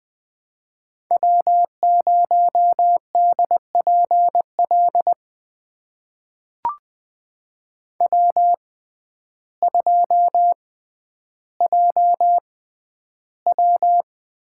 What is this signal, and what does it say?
Morse code, heard as a keyed tone, about 20 wpm, spaced as for 11 wpm, 700 Hz. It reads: W0DPL W2JW
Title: USA Call Signs - 5 Characters - 6x Character Spacing - 20wpm